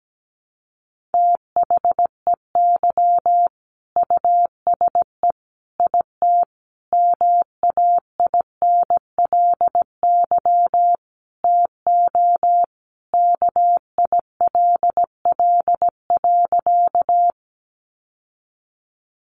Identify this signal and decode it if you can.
Morse code, heard as a keyed tone, about 17 wpm, 700 Hz. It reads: THEY USE IT MAINLY TO KILL.
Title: Sentences from Top 1400 Words 17wpm